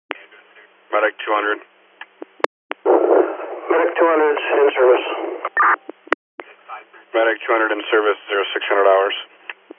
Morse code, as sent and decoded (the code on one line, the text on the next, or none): none